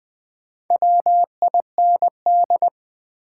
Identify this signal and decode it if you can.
Morse code, heard as a keyed tone, about 20 wpm, 700 Hz. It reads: WIND
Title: Sentences from Top 400 Words - New Word Drill - 20wpm